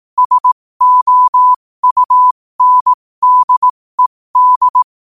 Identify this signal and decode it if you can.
Morse code, heard as a keyed tone, about 18 wpm, spaced as for 15 wpm, 1000 Hz. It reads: SOUNDED
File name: qotdfiles / 15wpm